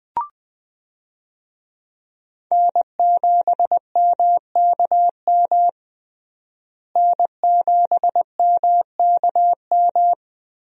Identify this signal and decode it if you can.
Morse code, heard as a keyed tone, about 20 wpm, 700 Hz. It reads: N7MKM N7MKM
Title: USA Call Signs - 5 Characters - Repeated 2x - 20wpm